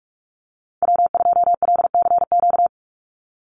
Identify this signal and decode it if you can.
Morse code, heard as a keyed tone, about 44 wpm, 700 Hz. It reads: W2LCQ
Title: Call Signs - 45wpm